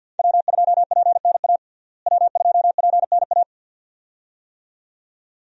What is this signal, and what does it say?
Morse code, heard as a keyed tone, about 48 wpm, 700 Hz. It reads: W2PNA W2PNA
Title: USA Call Signs - 5 Characters - Repeated 2x - 50wpm